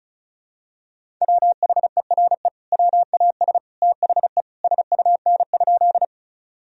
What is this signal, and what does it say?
Morse code, heard as a keyed tone, about 36 wpm, 700 Hz. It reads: WHERE WAS THE SUN?
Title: Sentences from Top 1000 Words - Review Part 1 - 35wpm